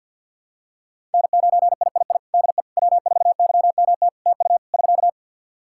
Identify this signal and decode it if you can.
Morse code, heard as a keyed tone, about 51 wpm, 700 Hz. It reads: N9III DE W4YMT TU <SK>
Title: Practice QSO - Exchange 4 - Ending - 50wpm